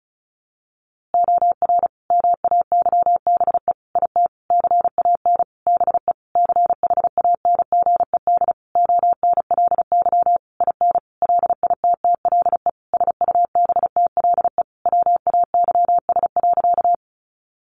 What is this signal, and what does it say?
Morse code, heard as a keyed tone, about 34 wpm, 700 Hz. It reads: OR MAYBE IT CAN BE CHANGED ONLY IN LITTLE SUBTLE WAYS.
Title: A Sound Of Thunder 35wpm